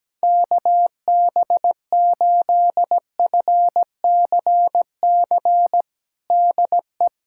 Morse code, heard as a keyed tone, about 17 wpm, 700 Hz. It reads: KB8FCC DE
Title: USA Callsign DE Callsign 17wpm